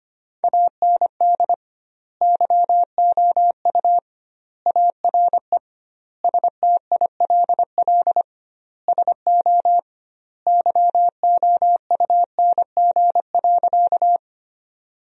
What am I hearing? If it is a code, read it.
Morse code, heard as a keyed tone, about 25 wpm, 700 Hz. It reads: AND YOU ARE STILL SO YOUNG.
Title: 2x Word-Spacing - Sentences from Top 400 Words - 25wpm